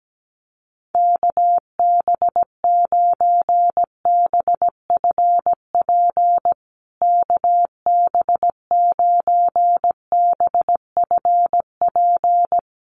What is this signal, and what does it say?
Morse code, heard as a keyed tone, about 17 wpm, 700 Hz. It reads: KB9BFP KB9BFP
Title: USA Call Signs Repeated 3x 17wpm